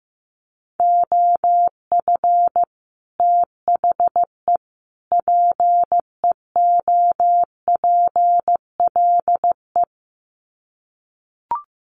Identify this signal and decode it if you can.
Morse code, heard as a keyed tone, about 15 wpm, 700 Hz. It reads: OF THE PEOPLE
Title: Sets of 3 Words 15wpm